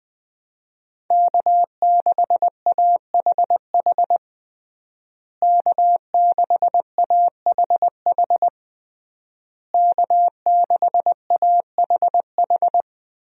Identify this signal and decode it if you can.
Morse code, heard as a keyed tone, about 21 wpm, 700 Hz. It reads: K6AHH K6AHH K6AHH
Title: USA Call Signs - 5 Characters - Repeated 3x - 20wpm